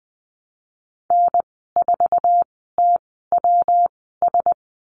Morse code, heard as a keyed tone, about 20 wpm, spaced as for 14 wpm, 700 Hz. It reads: N4TWS
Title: 140 Lesson 23g - Letter f - Call Signs 20wpm